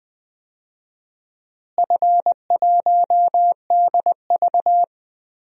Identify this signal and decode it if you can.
Morse code, heard as a keyed tone, about 20 wpm, 700 Hz. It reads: F1DV
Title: DXCC-100 Call Signs - 4 Characters - Repeated 2x - 20wpm